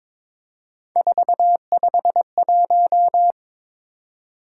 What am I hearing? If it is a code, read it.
Morse code, heard as a keyed tone, about 26 wpm, 700 Hz. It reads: 451
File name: Speed-Racing - Three Letter-Number Combinations - 22wpm